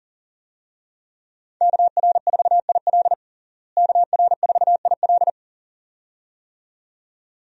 Morse code, heard as a keyed tone, about 40 wpm, 700 Hz. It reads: KR4IL KR4IL
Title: USA Call Signs - 5 Characters - Repeated 2x - 40wpm